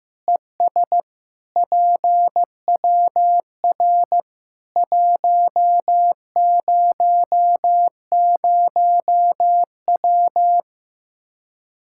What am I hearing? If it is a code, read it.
Morse code, heard as a keyed tone, about 15 wpm, 700 Hz. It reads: ES PWR 100W
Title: Practice QSO - Exchange 2 - Info - 15wpm